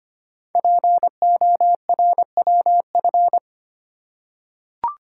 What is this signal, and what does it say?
Morse code, heard as a keyed tone, about 25 wpm, 700 Hz. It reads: PORWF
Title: Code Groups - 5 Letters - 25wpm